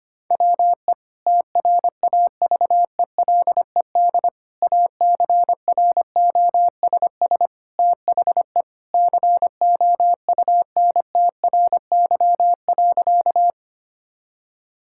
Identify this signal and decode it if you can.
Morse code, heard as a keyed tone, about 25 wpm, 700 Hz. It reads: WE TRAVELED ACROSS THE COUNTRY.
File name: practice / 25wpm / Sentences with 2-Word Phrases - 25wpm